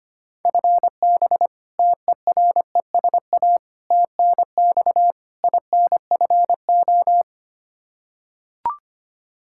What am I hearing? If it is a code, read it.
Morse code, heard as a keyed tone, about 25 wpm, 700 Hz. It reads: FB TERESA TNX INFO
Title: Practice QSO - Exchange 3 - More Info - 25wpm